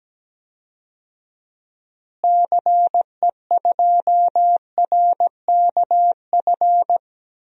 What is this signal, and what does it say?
Morse code, heard as a keyed tone, about 17 wpm, 700 Hz. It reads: CE2RKF
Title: DXCC-100 Call Signs - 6 Characters - Repeated 2x - 17wpm